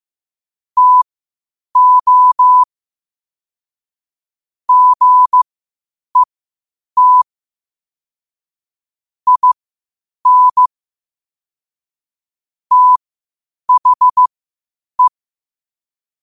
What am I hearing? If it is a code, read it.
Morse code, heard as a keyed tone, about 15 wpm, spaced as for 8 wpm, 1000 Hz. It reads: TO GET IN THE
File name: qotdfiles / 8wpm